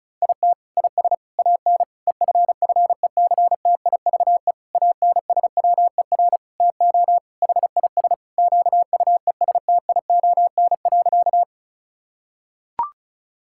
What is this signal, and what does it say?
Morse code, heard as a keyed tone, about 36 wpm, 700 Hz. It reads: IT IS AN EFFECTIVE ANSWER TO HIS QUESTION.
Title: Sentences from Top 1000 Words - Review Part 6 - 35wpm